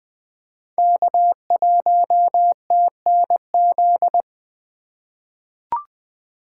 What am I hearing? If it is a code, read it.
Morse code, heard as a keyed tone, about 20 wpm, 700 Hz. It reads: K1TNZ
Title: USA Call Signs - 5 Characters - 2x Character Spacing - 20wpm